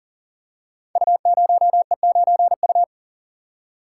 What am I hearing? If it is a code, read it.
Morse code, heard as a keyed tone, about 40 wpm, 700 Hz. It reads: U0E9U
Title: Code Groups - 5 Letter:Numbers - 40wpm